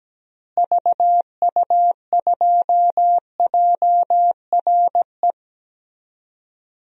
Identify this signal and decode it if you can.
Morse code, heard as a keyed tone, about 17 wpm, 700 Hz. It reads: VU2JRE
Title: DXCC-100 Call Signs - 6 Characters - Repeated 3x - 17wpm